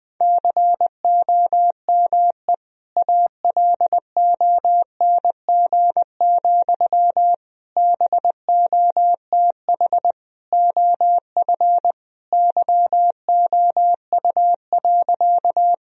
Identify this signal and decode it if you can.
Morse code, heard as a keyed tone, about 20 wpm, 700 Hz. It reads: COME ALONG, BOTH OF YOU.
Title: Sentences from Top 1000 Words - Review Part 1 - 20wpm